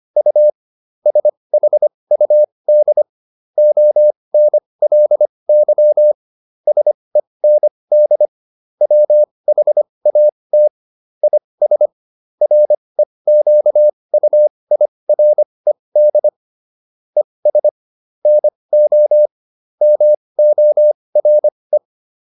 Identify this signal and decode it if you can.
Morse code, heard as a keyed tone, about 25 wpm, spaced as for 20 wpm, 600 Hz. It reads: U SHUD ONLY SEND WHAT IS REQUIRED ES NO MORE